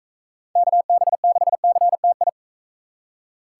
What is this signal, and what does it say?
Morse code, heard as a keyed tone, about 44 wpm, 700 Hz. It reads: KB6CTI